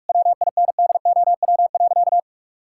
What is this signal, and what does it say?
Morse code, heard as a keyed tone, about 41 wpm, 700 Hz. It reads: WINDOW.
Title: Sentences from Top 1600 Words 45wpm